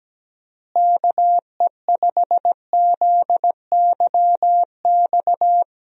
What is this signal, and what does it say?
Morse code, heard as a keyed tone, about 17 wpm, 700 Hz. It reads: KE5ZYX
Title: USA Call Signs - 6 Characters - Repeated 2x - 17wpm